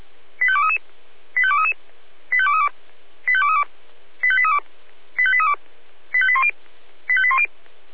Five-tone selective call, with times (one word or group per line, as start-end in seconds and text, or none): 0.4-0.8 85233
1.4-1.7 85233
2.3-2.7 85232
3.3-3.6 85232
4.2-4.6 85832
5.2-5.5 85832
6.1-6.5 85810
7.1-7.5 85810